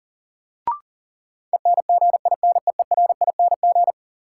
Morse code, heard as a keyed tone, about 39 wpm, 700 Hz. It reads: ENGINEERING
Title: Sentences from Top 1500 Words - New Word Drill Alt - 40wpm